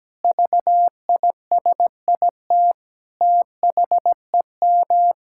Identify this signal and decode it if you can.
Morse code, heard as a keyed tone, about 17 wpm, 700 Hz. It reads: VISIT THEM
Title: Sentences from Top 1000 Words - Review Part 6 - 17wpm